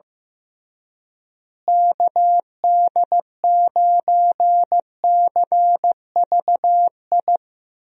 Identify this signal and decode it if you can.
Morse code, heard as a keyed tone, about 15 wpm, 700 Hz. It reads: KD9CVI
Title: USA Call Signs - 6 Characters - Repeated 3x - 15wpm